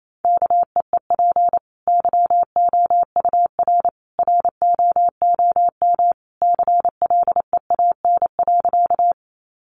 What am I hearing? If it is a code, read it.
Morse code, heard as a keyed tone, about 28 wpm, 700 Hz. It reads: KEEP YOUR ROOM CLEAN.
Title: Sentences from Top 1100 Words 28wpm